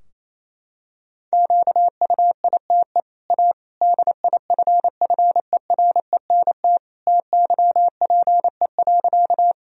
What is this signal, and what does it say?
Morse code, heard as a keyed tone, about 27 wpm, 690 Hz. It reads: QUITE A DIFFERENT TYPE.
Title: Sentences from Top 1000 Words - Review Part 3 - 28wpm